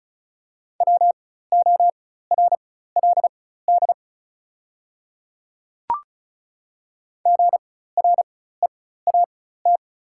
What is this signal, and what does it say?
Morse code, heard as a keyed tone, about 36 wpm, spaced as for 16 wpm, 700 Hz. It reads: WORLD GREAT
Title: 4x Character Spacing - Top 500 Words - 35wpm